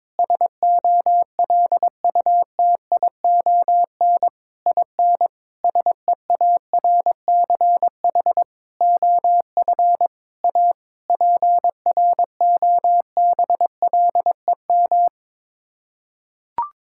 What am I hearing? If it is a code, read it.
Morse code, heard as a keyed tone, about 22 wpm, 700 Hz. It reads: SOLUTION IN SEARCH OF A PROBLEM
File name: practice / 22wpm / Sets of 8 Words - 22wpm